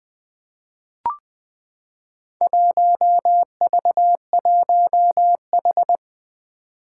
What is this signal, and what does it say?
Morse code, heard as a keyed tone, about 20 wpm, 700 Hz. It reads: 1V1H
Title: Four Challenging Letter-Number Combinations 20wpm